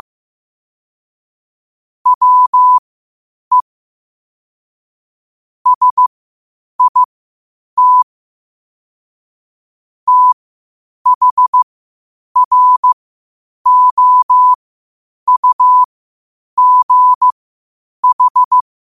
Morse code, heard as a keyed tone, about 15 wpm, spaced as for 8 wpm, 1000 Hz. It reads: WE SIT THROUGH
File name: qotdfiles / 8wpm